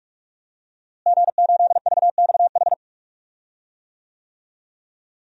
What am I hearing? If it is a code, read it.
Morse code, heard as a keyed tone, about 42 wpm, 700 Hz. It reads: G8VXH
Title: DXCC-100 Call Signs - 5 Characters - Repeated 2x - 45wpm